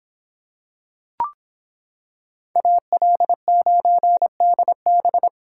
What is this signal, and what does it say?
Morse code, heard as a keyed tone, about 26 wpm, 700 Hz. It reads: AL9DB